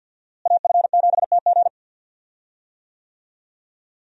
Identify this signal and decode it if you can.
Morse code, heard as a keyed tone, about 53 wpm, 700 Hz. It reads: AF7TG
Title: USA Call Signs - 5 Characters - Repeated 3x - 50wpm